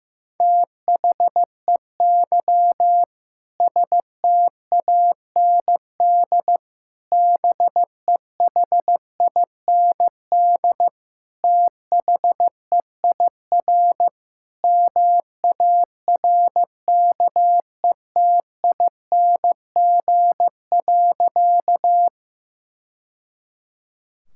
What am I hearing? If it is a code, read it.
Morse code, heard as a keyed tone, about 15 wpm, 700 Hz. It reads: THEY STAND BEHIND THEIR MARKETING.
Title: Sentences from Top 1000 Words - Review Part 5 - 15wpm